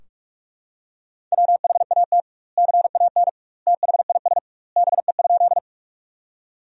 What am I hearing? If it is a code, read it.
Morse code, heard as a keyed tone, about 46 wpm, 700 Hz. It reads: WHAT CAN THIS BE?
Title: 2x Word-Spacing - Sentences from Top 100 Words - 45wpm